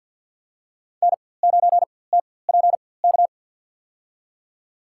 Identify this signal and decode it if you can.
Morse code, heard as a keyed tone, about 52 wpm, spaced as for 23 wpm, 700 Hz. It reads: N9TPK
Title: USA Call Signs - 5 Characters - 4x Character Spacing - 50wpm